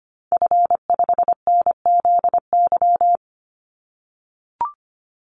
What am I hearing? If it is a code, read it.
Morse code, heard as a keyed tone, about 25 wpm, 700 Hz. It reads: F5NZY
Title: Call Signs - 25wpm